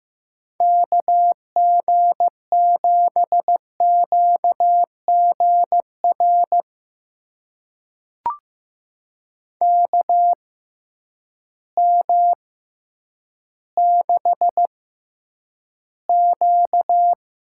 Morse code, heard as a keyed tone, about 15 wpm, spaced as for 9 wpm, 700 Hz. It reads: KG7QGR KM6Q
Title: USA Call Signs - 6 Characters - 6x Character Spacing - 15wpm